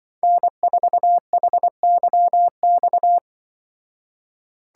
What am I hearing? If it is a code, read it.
Morse code, heard as a keyed tone, about 24 wpm, 700 Hz. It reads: N4HYX
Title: Speed Racing - USA 1x3 Call Signs 20wpm